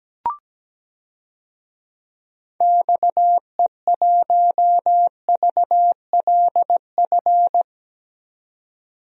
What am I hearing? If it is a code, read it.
Morse code, heard as a keyed tone, about 17 wpm, 700 Hz. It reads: XE1VLF
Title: DXCC-100 Call Signs - 6 Characters - 17wpm